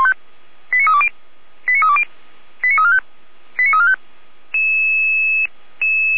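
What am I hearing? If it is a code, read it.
Five-tone selective call (ZVEI), heard as quick runs of five tones, tones of about 70 ms. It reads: 89320 89320 89345 89345
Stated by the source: ZVEI five-tone selective call 89081 received via SDR, archived on the Feuerwehr Orlamünde (f85360) audio server.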